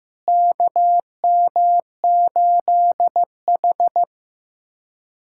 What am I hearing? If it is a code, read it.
Morse code, heard as a keyed tone, about 15 wpm, 700 Hz. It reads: KM8H
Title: USA Callsign DE Callsign - 4 Characters - 15wpm